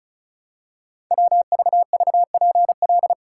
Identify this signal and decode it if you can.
Morse code, heard as a keyed tone, about 36 wpm, 700 Hz. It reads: WVVPL
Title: Code Groups - 5 Letter:Numbers - 35wpm